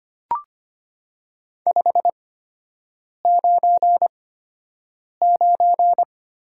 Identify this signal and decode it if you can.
Morse code, heard as a keyed tone, about 25 wpm, spaced as for 7 wpm, 700 Hz. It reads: 599
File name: Top 90 QSO Elements - 8x Character Spacing - 25wpm